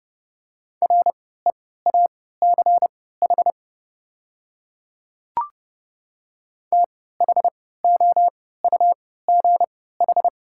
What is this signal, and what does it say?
Morse code, heard as a keyed tone, about 30 wpm, spaced as for 17 wpm, 690 Hz. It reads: REACH THOUGH
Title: 3x Character Spacing - Top 500 Words - 30wpm